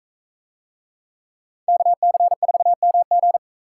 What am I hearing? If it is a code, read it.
Morse code, heard as a keyed tone, about 41 wpm, 700 Hz. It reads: KC4MG